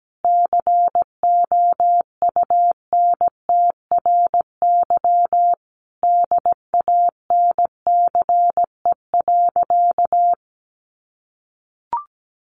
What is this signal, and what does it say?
Morse code, heard as a keyed tone, about 17 wpm, 700 Hz. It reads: COUNTRY DANCE.
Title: Sentences from Top 1300 Words 17wpm